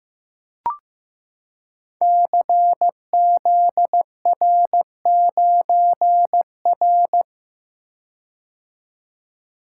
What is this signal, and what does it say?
Morse code, heard as a keyed tone, about 15 wpm, 700 Hz. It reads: CZR9R
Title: Code Groups - 5 Letter:Numbers - 15wpm